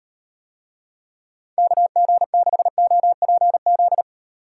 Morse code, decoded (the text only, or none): KG6OPZ